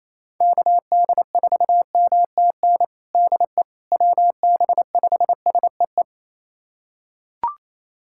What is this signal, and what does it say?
Morse code, heard as a keyed tone, about 28 wpm, 700 Hz. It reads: KD4MTN DE WB5SEE
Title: USA Callsign DE Callsign - 6 Characters - 28wpm